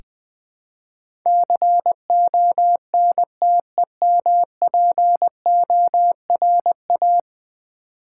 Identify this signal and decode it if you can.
Morse code, heard as a keyed tone, about 20 wpm, 700 Hz. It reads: CONTEMPORA
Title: Build A Word - 5 Letters At A Time - 12-Letter Words - 20wpm